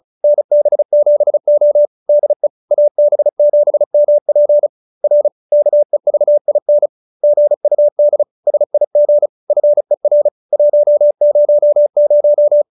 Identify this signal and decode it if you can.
Morse code, heard as a keyed tone, about 35 wpm, 590 Hz. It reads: NB7O DE AB7MP R KEVIN GUD SIG FER 100